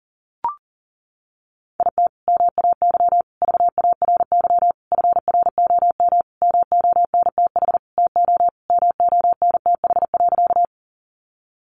Morse code, heard as a keyed tone, about 40 wpm, 700 Hz. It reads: IT MAY VARY FROM MONTH TO MONTH.
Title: Sentences from Top 1200 Words 40wpm